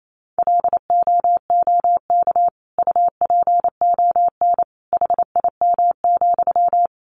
Morse code, heard as a keyed tone, about 28 wpm, 700 Hz. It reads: LOOK UPON HIM,
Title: Sentences from Top 1100 Words 28wpm